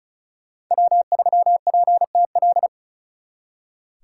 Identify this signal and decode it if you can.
Morse code, heard as a keyed tone, about 46 wpm, 700 Hz. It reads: W3PTL